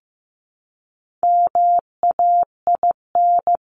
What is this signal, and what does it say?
Morse code, heard as a keyed tone, about 15 wpm, 700 Hz. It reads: MAIN
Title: Top 300 Words 15wpm